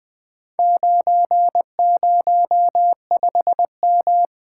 Morse code, heard as a keyed tone, about 20 wpm, 700 Hz. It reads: 905M
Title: Four Letter-Number Combinations - 20wpm